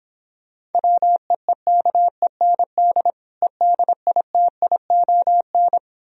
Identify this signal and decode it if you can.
Morse code, heard as a keyed tone, about 29 wpm, 700 Hz. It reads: WEEKEND EDITION